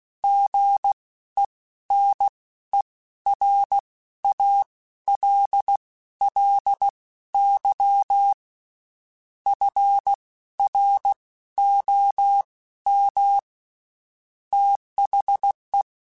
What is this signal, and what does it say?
Morse code, heard as a keyed tone, about 16 wpm, spaced as for 12 wpm, 800 Hz. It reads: GENERALLY FROM THE
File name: Spaced Characters / Spaced Characters-mp3s / Weather 5 22 sp @15